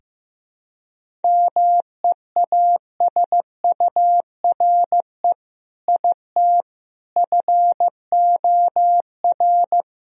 Morse code, heard as a keyed tone, about 15 wpm, 700 Hz. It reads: MEASURE IT FOR
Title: Sentences from Top 1000 Words - Review Part 2 - 15wpm